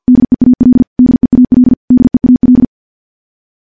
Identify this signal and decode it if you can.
Morse code, heard as a keyed tone, about 47 wpm, 250 Hz. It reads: DEAL DEAL DEAL
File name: morse / morse-training-50WPM